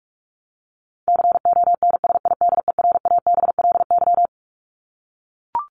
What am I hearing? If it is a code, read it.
Morse code, heard as a keyed tone, about 44 wpm, 700 Hz. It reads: CONSIDERABLY
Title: LY Suffix - Top 100 Words with Base Words 45wpm